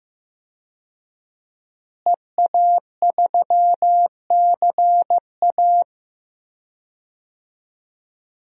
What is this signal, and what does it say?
Morse code, heard as a keyed tone, about 16 wpm, 700 Hz. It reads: EA3CA
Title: DXCC-100 Call Signs - 5 Characters - Repeated 3x - 15wpm